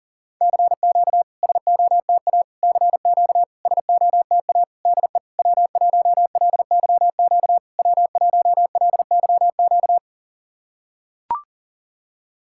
Morse code, heard as a keyed tone, about 40 wpm, 700 Hz. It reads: CQ SOTA CQ SOTA DE W1LYQ W1LYQ
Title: SOTA QSO Practice 40wpm